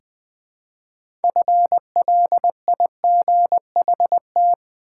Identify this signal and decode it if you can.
Morse code, heard as a keyed tone, about 20 wpm, 700 Hz. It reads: FLIGHT
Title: Six Letter Words - 20wpm